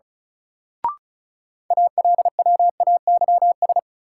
Morse code, heard as a keyed tone, about 35 wpm, 700 Hz. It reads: ALWAYS
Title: Sentences from Top 300 Words - New Word Drill - 35wpm